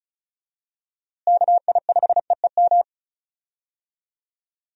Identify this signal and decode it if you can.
Morse code, heard as a keyed tone, about 36 wpm, 700 Hz. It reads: KI5EEM
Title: USA Call Signs - 6 Characters - Repeated 2x - 35wpm